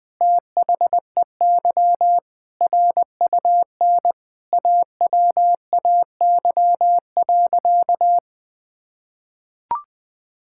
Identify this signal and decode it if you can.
Morse code, heard as a keyed tone, about 19 wpm, 700 Hz. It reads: THEY RUN AWAY.
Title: Sentences from Top 1600 Words 20wpm